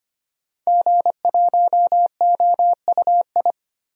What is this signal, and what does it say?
Morse code, heard as a keyed tone, about 25 wpm, 700 Hz. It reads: G1OUI
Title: DXCC-100 Call Signs - 5 Characters - Repeated 2x - 25wpm